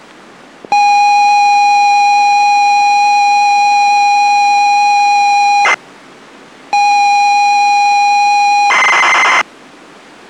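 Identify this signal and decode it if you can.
ZX Spectrum tape loading.
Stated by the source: ZX Spectrum tape loading from the game Deep Zone, dumped from the original cassette.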